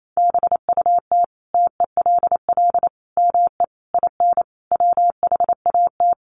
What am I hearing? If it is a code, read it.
Morse code, heard as a keyed tone, about 28 wpm, 700 Hz. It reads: BUT TELL ME IN WHAT
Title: Sentences from Top 1300 Words 28wpm